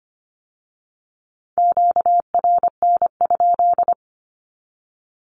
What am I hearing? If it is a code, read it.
Morse code, heard as a keyed tone, about 25 wpm, 700 Hz. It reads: QRN?
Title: Q-Codes with Spoken Meaning - 25wpm